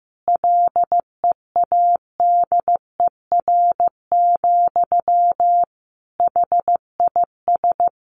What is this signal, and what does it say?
Morse code, heard as a keyed tone, about 15 wpm, 700 Hz. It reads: LEADER, HIS
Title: A Sound Of Thunder 15wpm